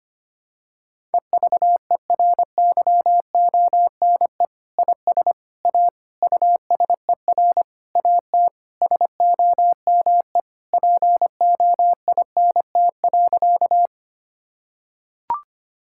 Morse code, heard as a keyed tone, about 25 wpm, 700 Hz. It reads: EVERYONE IS A USER AT SOME POINT.
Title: Sentences from Top 1000 Words - Review Part 5 - 25wpm